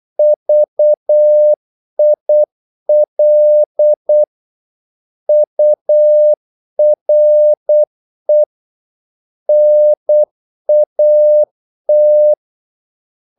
Morse code, heard as a keyed tone, about 8 wpm, 600 Hz. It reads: VIL URE NAT
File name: Korte ord DK 8wpm